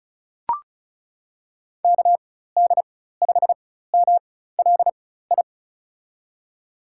Morse code, heard as a keyed tone, about 35 wpm, spaced as for 16 wpm, 700 Hz. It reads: KD5MLI
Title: USA Call Signs - 4x Character Spacing - 35wpm